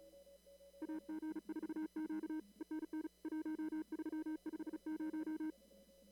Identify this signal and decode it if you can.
Morse code, heard as a keyed tone, about 37 wpm, 310 Hz. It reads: AG4Q ENN 1350
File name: CW / kt5j wpx cw-op-k5pi